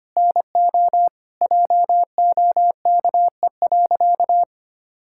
Morse code, heard as a keyed tone, about 25 wpm, 700 Hz. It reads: NO JOKE.
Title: Pangram Challenge v2 - 25wpm